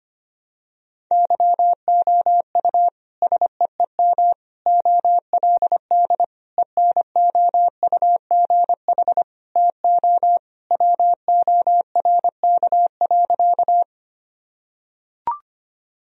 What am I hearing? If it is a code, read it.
Morse code, heard as a keyed tone, about 25 wpm, 700 Hz. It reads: YOU SEEM OLD ENOUGH TO WORK.
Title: Sentences from Top 1000 Words - Review Part 1 - 25wpm